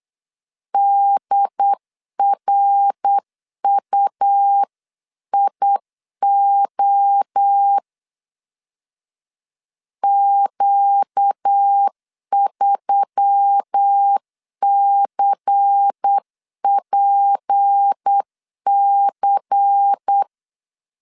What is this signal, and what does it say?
Morse code, heard as a keyed tone, about 8 wpm, 800 Hz. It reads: DRUIO Q3CPC
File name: CW / DET / 1 6 1500C S C 8wpm 5219